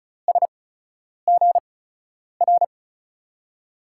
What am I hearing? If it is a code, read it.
Morse code, heard as a keyed tone, about 36 wpm, spaced as for 10 wpm, 700 Hz. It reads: SGR
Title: USA Call Signs - 8x Character Spacing - 35wpm